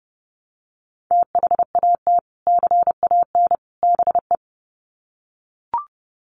Morse code, heard as a keyed tone, about 30 wpm, 700 Hz. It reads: THAT CAN BE
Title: Sets of 3 Words 30wpm